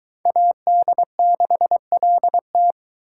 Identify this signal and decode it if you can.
Morse code, heard as a keyed tone, about 22 wpm, 700 Hz. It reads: AD6LT